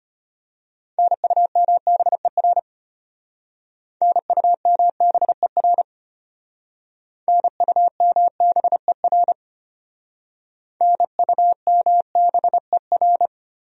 Morse code, heard as a keyed tone, about 32 wpm, 700 Hz. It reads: NUMBER NUMBER NUMBER NUMBER